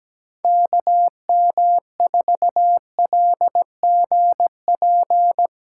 Morse code, heard as a keyed tone, about 17 wpm, 700 Hz. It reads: KM4LGP
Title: USA Call Signs - 2x Character Spacing - 17wpm